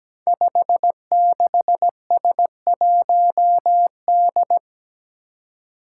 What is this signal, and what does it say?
Morse code, heard as a keyed tone, about 17 wpm, 700 Hz. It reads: 56S1D
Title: Five Challenging Letter-Number Combinations 17wpm